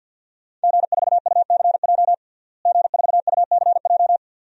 Morse code, heard as a keyed tone, about 51 wpm, 700 Hz. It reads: G4UCJ G4UCJ
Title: DXCC-100 Call Signs - 5 Characters - Repeated 2x - 50wpm